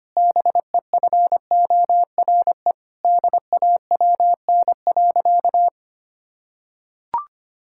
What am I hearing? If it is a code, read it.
Morse code, heard as a keyed tone, about 25 wpm, 700 Hz. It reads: BEFORE DAWN.
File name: practice / 25wpm / Difficult Sentences with Hints - 25wpm